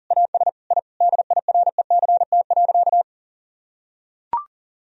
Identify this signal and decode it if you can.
Morse code, heard as a keyed tone, about 37 wpm, 700 Hz. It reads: AS I DIRECT.
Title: Sentences from Top 1000 Words - Review Part 2 - 40wpm